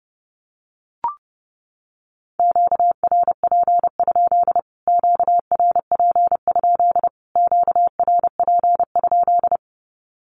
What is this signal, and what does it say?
Morse code, heard as a keyed tone, about 30 wpm, 700 Hz. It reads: QRP? QRP? QRP?
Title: Q-Codes Repeated 3x 30wpm